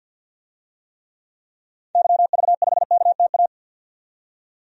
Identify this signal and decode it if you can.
Morse code, heard as a keyed tone, about 48 wpm, 700 Hz. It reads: YV5KTA
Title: DXCC-100 Call Signs - 6 Characters - 50wpm